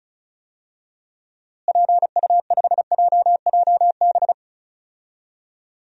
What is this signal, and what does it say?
Morse code, heard as a keyed tone, about 36 wpm, 700 Hz. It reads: PU5JJB
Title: DXCC-100 Call Signs - 6 Characters - Repeated 3x - 35wpm